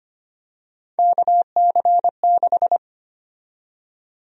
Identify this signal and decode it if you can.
Morse code, heard as a keyed tone, about 25 wpm, 700 Hz. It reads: KC6
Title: USA Call Sign Prefixes - 25wpm